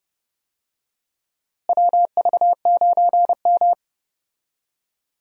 Morse code, heard as a keyed tone, about 31 wpm, 700 Hz. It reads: WV9M
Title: USA Call Signs - 4 Characters - Repeated 2x - 30wpm